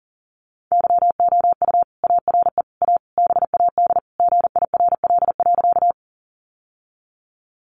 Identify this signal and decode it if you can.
Morse code, heard as a keyed tone, about 40 wpm, 700 Hz. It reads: YOU ARE A BAD GIRL.
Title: Sentences from Top 700 Words 40wpm